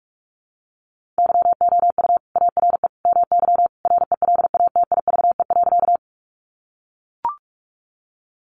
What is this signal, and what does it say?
Morse code, heard as a keyed tone, about 44 wpm, 700 Hz. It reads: YOU ARE MY RELATIVE.
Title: Sentences from Top 1400 Words 45wpm